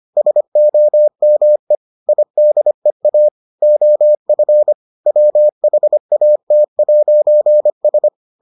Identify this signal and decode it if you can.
Morse code, heard as a keyed tone, about 25 wpm, 600 Hz. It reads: SOME IDEA OF WHAT'S